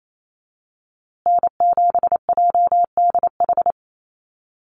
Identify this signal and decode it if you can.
Morse code, heard as a keyed tone, about 28 wpm, 700 Hz. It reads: N7JDH
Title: USA Call Signs Repeated 2x 28wpm